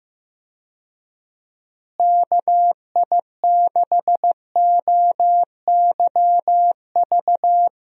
Morse code, heard as a keyed tone, about 15 wpm, 700 Hz. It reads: KI6OYV
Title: USA Call Signs - 6 Characters - Repeated 2x - 15wpm